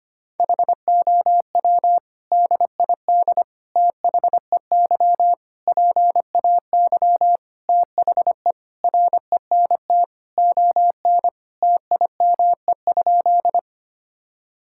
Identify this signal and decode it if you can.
Morse code, heard as a keyed tone, about 25 wpm, 700 Hz. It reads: HOW DID THEY PAY THE RENT ON TIME?
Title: Sentences from Top 1000 Words - Review Part 5 - 25wpm